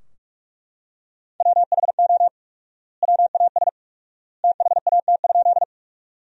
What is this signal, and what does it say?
Morse code, heard as a keyed tone, about 42 wpm, 700 Hz. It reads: WHO WAS THAT?
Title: 4x Word-Spacing - Sentences from Top 100 Words - 45wpm